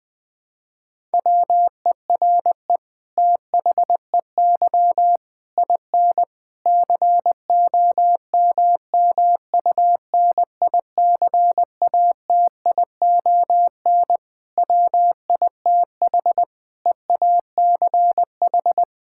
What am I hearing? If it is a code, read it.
Morse code, heard as a keyed tone, about 20 wpm, 700 Hz. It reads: WERE THEY IN COMMUNICATION WITH EACH
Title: Sentences from Top 1000 Words - Review Part 6 - 20wpm